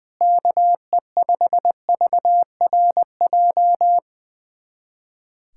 Morse code, heard as a keyed tone, about 20 wpm, 700 Hz. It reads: KE5VRJ
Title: USA Callsign DE Callsign - 6 Characters - 20wpm